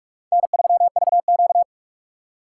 Morse code, heard as a keyed tone, about 44 wpm, 700 Hz. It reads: N3VQ
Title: USA Call Signs - 4 Characters - 2x Character Spacing - 45wpm